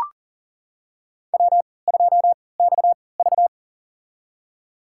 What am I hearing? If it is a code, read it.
Morse code, heard as a keyed tone, about 40 wpm, spaced as for 23 wpm, 700 Hz. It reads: W2XV
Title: USA Call Signs - 4 Characters - 3x Character Spacing - 40wpm